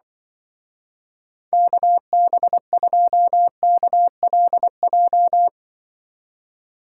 Morse code, heard as a keyed tone, about 24 wpm, 700 Hz. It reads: KB2KLJ
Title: Speed Racing - USA 2x3 Call Signs 20wpm